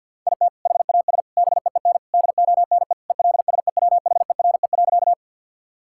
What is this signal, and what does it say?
Morse code, heard as a keyed tone, about 50 wpm, 700 Hz. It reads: IT HAS BEEN DONE ELSEWHERE.
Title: Sentences from Top 1600 Words 50wpm